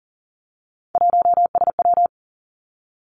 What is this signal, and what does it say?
Morse code, heard as a keyed tone, about 41 wpm, 700 Hz. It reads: 1SW
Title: Three Challenging Letter-Number Combinations 40wpm